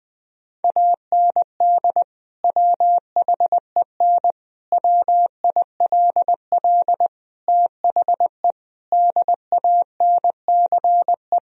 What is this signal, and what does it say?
Morse code, heard as a keyed tone, about 20 wpm, 700 Hz. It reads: AND WHEN WILL THE DANCE
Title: Sentences from Top 1000 Words - Review Part 4 - 20wpm